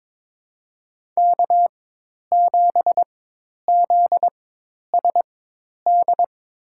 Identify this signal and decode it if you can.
Morse code, heard as a keyed tone, about 22 wpm, spaced as for 10 wpm, 700 Hz. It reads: K7ZSD
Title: USA Call Signs - 5 Characters - 4x Character Spacing - 22wpm